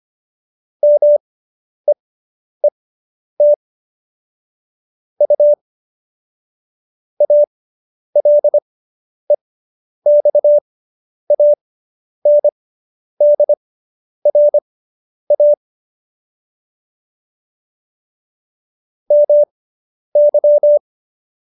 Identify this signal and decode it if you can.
Morse code, heard as a keyed tone, about 25 wpm, spaced as for 10 wpm, 590 Hz. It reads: MEET U ALEXANDRA MY